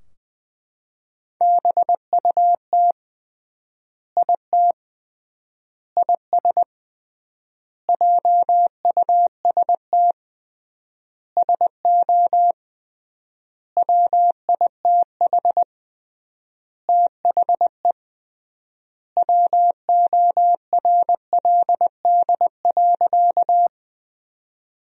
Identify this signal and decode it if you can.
Morse code, heard as a keyed tone, about 20 wpm, 700 Hz. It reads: BUT IT IS JUST SO WITH THE WORLD.
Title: 3x Word-Spacing - Sentences from Top 200 Words - 20wpm